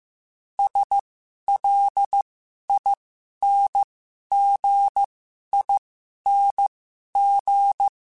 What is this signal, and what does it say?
Morse code, heard as a keyed tone, about 15 wpm, spaced as for 11 wpm, 800 Hz. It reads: SLINGING
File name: Special INGs / Special INGs-mp3s / Dble ings @10